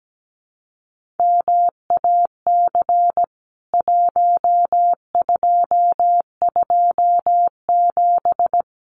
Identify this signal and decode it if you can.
Morse code, heard as a keyed tone, about 17 wpm, 700 Hz. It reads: MAC 1227
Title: CWT Exchange 17wpm